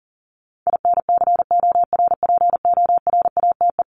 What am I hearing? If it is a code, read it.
Morse code, heard as a keyed tone, about 40 wpm, 700 Hz. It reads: INCORPORATE
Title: Top 1751-2000 Words 40wpm